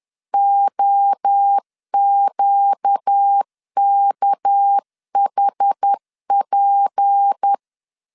Morse code, heard as a keyed tone, about 10 wpm, 800 Hz. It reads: OQKHP
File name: LEZIONI / 6 PKX0 / 6 PK 10wpm 2436